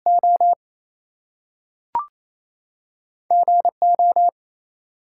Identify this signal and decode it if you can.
Morse code, heard as a keyed tone, about 28 wpm, 700 Hz. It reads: O GO
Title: Two Letter Words - 28wpm